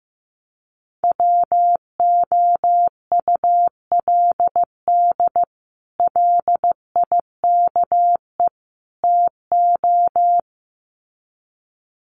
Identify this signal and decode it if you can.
Morse code, heard as a keyed tone, about 15 wpm, 700 Hz. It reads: WOULD LIKE TO
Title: Sets of 3 Words 15wpm